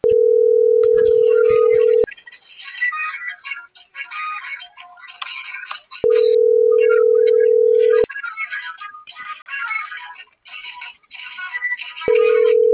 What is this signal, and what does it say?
Ringback tone (North America), 440 + 480 Hz.